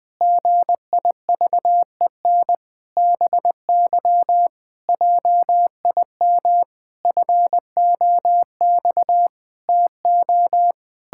Morse code, heard as a keyed tone, about 21 wpm, 700 Hz. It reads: GIVEN BY JIM FOX TO
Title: Pangram Challenge v2 - 20wpm